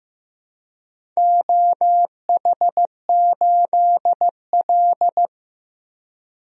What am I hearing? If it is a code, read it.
Morse code, heard as a keyed tone, about 15 wpm, 700 Hz. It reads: OH8L
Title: DXCC-100 Call Signs - 4 Characters - Repeated 3x - 15wpm